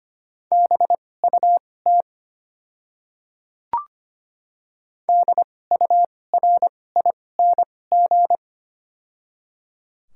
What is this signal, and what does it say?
Morse code, heard as a keyed tone, about 25 wpm, spaced as for 18 wpm, 700 Hz. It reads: BUT DURING
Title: 2x Character Spacing - Top 500 Words - 25wpm